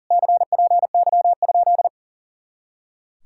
Morse code, heard as a keyed tone, about 52 wpm, 700 Hz. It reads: CPY?